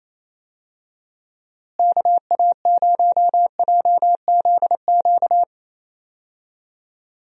Morse code, heard as a keyed tone, about 28 wpm, 700 Hz. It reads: KA0JZQ
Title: USA Call Signs - 6 Characters - Repeated 3x - 28wpm